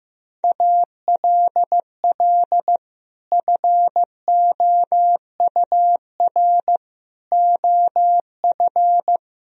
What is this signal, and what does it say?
Morse code, heard as a keyed tone, about 16 wpm, 700 Hz. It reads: ALL FOUR OF
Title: Sentences from Top 1000 Words - Review Part 1 - 15wpm